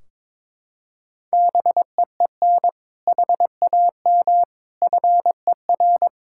Morse code, heard as a keyed tone, about 22 wpm, 700 Hz. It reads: BEEN HAM FER
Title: Practice QSO - Exchange 3 - More Info - 22wpm